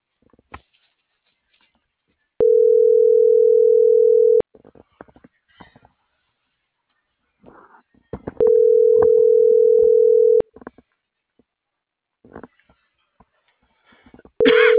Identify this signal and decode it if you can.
Ringback tone (North America), 440 + 480 Hz.